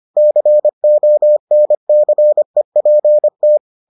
Morse code, heard as a keyed tone, about 25 wpm, 600 Hz. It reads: CONCEPT